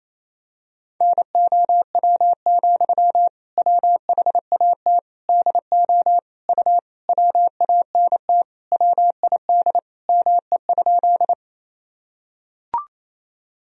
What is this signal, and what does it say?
Morse code, heard as a keyed tone, about 27 wpm, 700 Hz. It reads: NOW, WHAT DO U WANT WID ME?
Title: Sentences from Top 200 Words and Abbreviations 28wpm